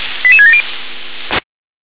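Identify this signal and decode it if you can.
Five-tone selective call (ZVEI), heard as quick runs of five tones, tones of about 80 ms. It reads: 88670
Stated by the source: ZVEI five-tone selective call received on SDR.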